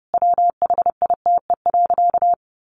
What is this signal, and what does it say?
Morse code, heard as a keyed tone, about 30 wpm, 700 Hz. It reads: WHITE.
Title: Sentences from Top 1200 Words 30wpm